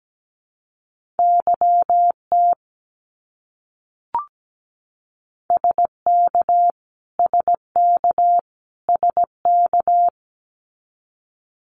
Morse code, heard as a keyed tone, about 17 wpm, 700 Hz. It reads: YT SK SK SK
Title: Canadian Provinces and Territories - Repeated 3x - 17wpm